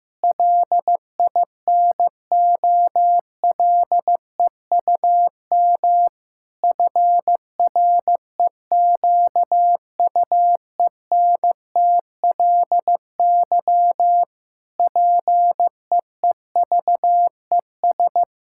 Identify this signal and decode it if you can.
Morse code, heard as a keyed tone, about 15 wpm, 700 Hz. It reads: LINOLEUM FREQUENTLY PEEVES
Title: Pangram Challenge v2 - 15wpm